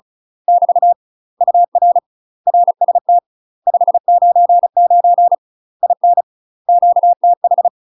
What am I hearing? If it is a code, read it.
Morse code, heard as a keyed tone, about 36 wpm, 700 Hz. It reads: <BT> UR RST 599 IN QTH
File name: UK MASTER QSOs TRAINING / 35WPM / UK MASTER QSOs 0016